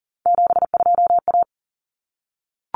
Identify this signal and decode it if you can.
Morse code, heard as a keyed tone, about 40 wpm, 700 Hz. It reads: 72A
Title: Common Cut Numbers - 3-Digit 40wpm